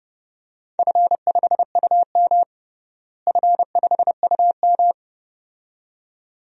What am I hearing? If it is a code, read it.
Morse code, heard as a keyed tone, about 31 wpm, 700 Hz. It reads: F5UM F5UM
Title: DXCC-100 Call Signs - 4 Characters - Repeated 2x - 30wpm